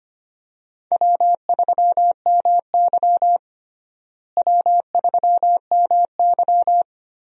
Morse code, heard as a keyed tone, about 25 wpm, 700 Hz. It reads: W3MY W3MY
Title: USA Call Signs - 4 Characters - Repeated 3x - 25wpm